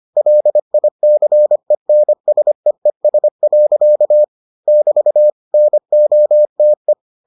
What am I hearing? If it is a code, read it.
Morse code, heard as a keyed tone, about 25 wpm, 600 Hz. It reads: LICENSEES. <BT> NOTE